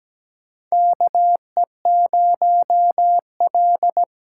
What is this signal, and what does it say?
Morse code, heard as a keyed tone, about 17 wpm, 700 Hz. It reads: KE0L
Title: USA Call Signs - 6 Characters - 8x Character Spacing - 17wpm